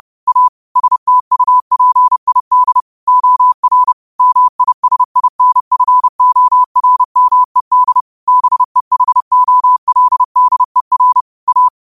Morse code, heard as a keyed tone, about 30 wpm, 1000 Hz. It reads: A STUPID OR MISINFORMED BEHOLDER A
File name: qotdfiles / 30wpm